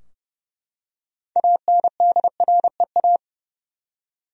Morse code, heard as a keyed tone, about 32 wpm, 700 Hz. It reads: ANDREA
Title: USA Common Names - 30wpm